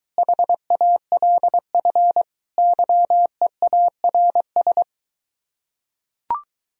Morse code, heard as a keyed tone, about 23 wpm, 700 Hz. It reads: HALF YEARS